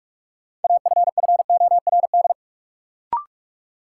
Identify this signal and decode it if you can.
Morse code, heard as a keyed tone, about 45 wpm, 700 Hz. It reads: AFFORD
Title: Sentences from Top 1100 Words - New Word Drill Alt - 45wpm